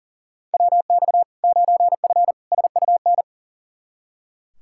Morse code, heard as a keyed tone, about 40 wpm, 700 Hz. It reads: WX 9F SUN
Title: Practice QSO - Exchange 2 - Info - 40wpm